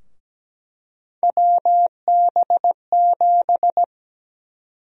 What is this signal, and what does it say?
Morse code, heard as a keyed tone, about 17 wpm, 700 Hz. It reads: WB7
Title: USA Call Sign Prefixes - 17wpm